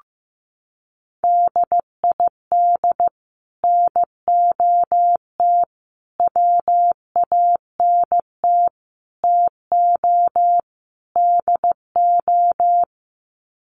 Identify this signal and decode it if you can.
Morse code, heard as a keyed tone, about 15 wpm, 700 Hz. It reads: DID NOT WANT TO DO
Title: Sets of 5 Words Encore 15wpm